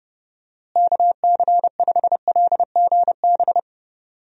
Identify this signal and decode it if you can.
Morse code, heard as a keyed tone, about 32 wpm, 700 Hz. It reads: KC5LGB
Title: USA Call Signs - 6 Characters - Repeated 3x - 30wpm